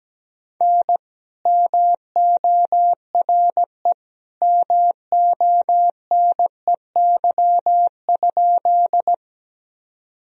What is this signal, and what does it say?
Morse code, heard as a keyed tone, about 17 wpm, 700 Hz. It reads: N MORE MONEY?
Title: Sentences from Top 400 Words and Abbreviations 17wpm